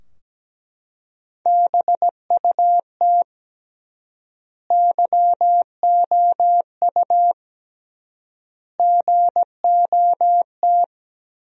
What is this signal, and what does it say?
Morse code, heard as a keyed tone, about 17 wpm, 700 Hz. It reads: BUT YOU GOT
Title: 3x Word-Spacing - Sentences from Top 300 Words - 17wpm